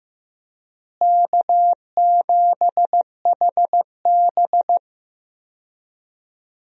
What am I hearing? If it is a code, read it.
Morse code, heard as a keyed tone, about 15 wpm, 700 Hz. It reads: K7HB
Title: USA Call Signs - 4 Characters - Repeated 3x - 15wpm